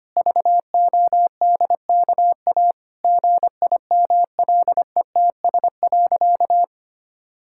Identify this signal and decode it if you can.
Morse code, heard as a keyed tone, about 25 wpm, 700 Hz. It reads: VODKA GIMLETS.
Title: Pangram Challenge v2 - 25wpm